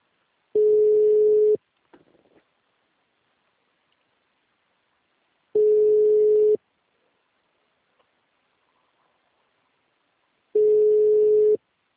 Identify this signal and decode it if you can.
Ringback tone (Europe, 425 Hz).
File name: out- -2002- - - .11412